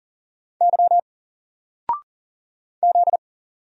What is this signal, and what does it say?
Morse code, heard as a keyed tone, about 41 wpm, 700 Hz. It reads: Y Z
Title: Single Letter-Number Alt 40wpm